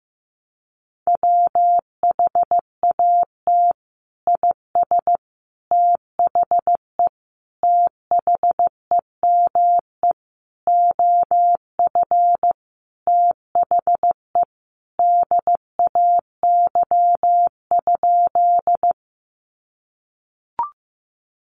Morse code, heard as a keyed tone, about 15 wpm, 700 Hz. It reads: WHAT IS THE THEME OF THE DAY?
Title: Sentences from Top 1100 Words 15wpm